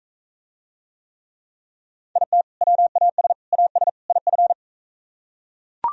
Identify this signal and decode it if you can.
Morse code, heard as a keyed tone, about 43 wpm, 700 Hz. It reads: IT WAS AS IF